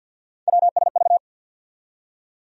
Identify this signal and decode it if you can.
Morse code, heard as a keyed tone, about 47 wpm, 700 Hz. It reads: WSV
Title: USA Call Sign Postfixes - 3 Letters 50wpm